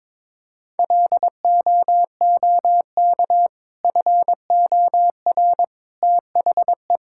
Morse code, heard as a keyed tone, about 22 wpm, 700 Hz. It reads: LOOK FOR THE
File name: practice / 22wpm / Sentences with 2-Word Phrases - 22wpm